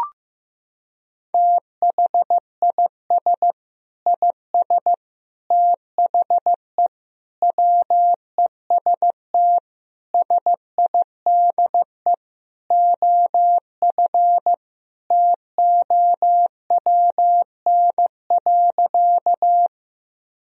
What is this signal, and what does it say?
Morse code, heard as a keyed tone, about 15 wpm, 700 Hz. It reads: THIS IS THE WEST SIDE OF TOWN.
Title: Sentences from Top 1000 Words - Review Part 2 - 15wpm